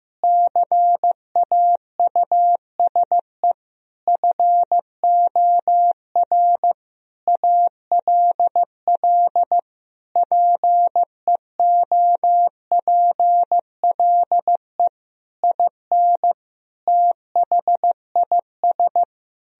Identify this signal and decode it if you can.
Morse code, heard as a keyed tone, about 15 wpm, 700 Hz. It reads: CAUSE FOR ALL PEOPLE IN THIS
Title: Sentences from Top 1000 Words - Review Part 3 - 15wpm